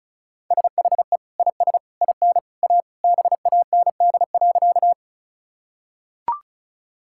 Morse code, heard as a keyed tone, about 36 wpm, 700 Hz. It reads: SHE IS IN A BAND.
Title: Sentences from Top 1600 Words 35wpm